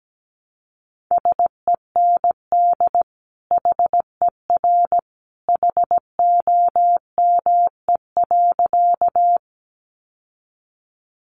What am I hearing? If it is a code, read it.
Morse code, heard as a keyed tone, about 17 wpm, 700 Hz. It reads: SEND HER HOME.
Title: Sentences from Top 1500 Words 17wpm